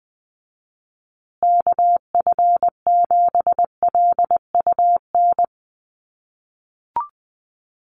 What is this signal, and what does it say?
Morse code, heard as a keyed tone, about 20 wpm, 700 Hz. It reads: KF7LUN
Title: USA Call Signs Repeated 3x 20wpm